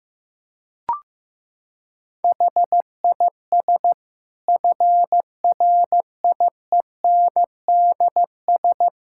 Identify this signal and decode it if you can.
Morse code, heard as a keyed tone, about 15 wpm, 700 Hz. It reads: HIS FRIENDS
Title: Sentences from Top 1600 Words 15wpm